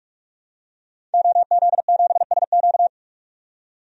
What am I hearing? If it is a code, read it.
Morse code, heard as a keyed tone, about 44 wpm, 690 Hz. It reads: OZ7SQ
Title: DXCC-100 Call Signs - 5 Characters - 45wpm